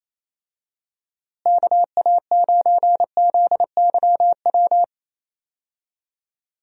Morse code, heard as a keyed tone, about 28 wpm, 700 Hz. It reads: KA9ZYW
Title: USA Call Signs - 6 Characters - Repeated 3x - 28wpm